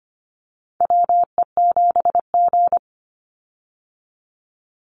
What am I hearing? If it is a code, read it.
Morse code, heard as a keyed tone, about 25 wpm, 700 Hz. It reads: WE7G
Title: Call Signs Repeated 3x - 25wpm